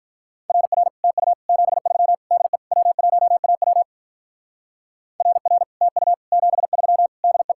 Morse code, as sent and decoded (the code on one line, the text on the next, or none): .-. .-. / - ..- / --... ...-- / -.. / .-- .---- .- .-- / .-. .-. / - ..- / --... ...-- / -.. .
RR TU 73 DE W1AW RR TU 73 DE